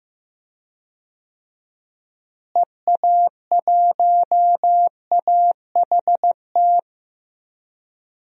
Morse code, heard as a keyed tone, about 15 wpm, 700 Hz. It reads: EA1AHT
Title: DXCC-100 Call Signs - 6 Characters - Repeated 2x - 15wpm